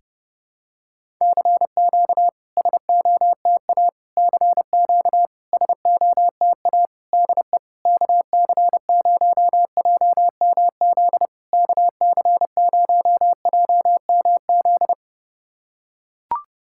Morse code, heard as a keyed tone, about 30 wpm, 700 Hz. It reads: CQ SOTA CQ SOTA DE KC0JMZ KC0JMZ
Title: SOTA QSO Practice 30wpm